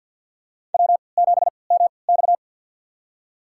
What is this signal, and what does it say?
Morse code, heard as a keyed tone, about 51 wpm, spaced as for 28 wpm, 700 Hz. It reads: W7MX
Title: USA Call Signs - 4 Characters - 3x Character Spacing - 50wpm